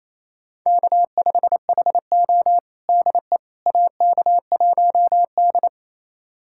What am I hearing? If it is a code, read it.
Morse code, heard as a keyed tone, about 28 wpm, 690 Hz. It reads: K5HO DE AK1D
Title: USA Callsign DE Callsign - 4 Characters - 28wpm